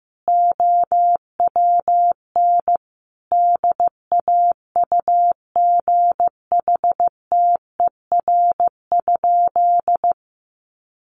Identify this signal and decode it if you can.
Morse code, heard as a keyed tone, about 15 wpm, 700 Hz. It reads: OWN DAUGHTER?
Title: Sentences from Top 1500 Words 15wpm